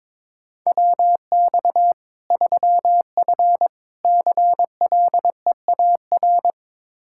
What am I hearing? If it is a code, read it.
Morse code, heard as a keyed tone, about 22 wpm, 700 Hz. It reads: WX 3F CLEAR
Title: K1USN SST QSO Practice 22wpm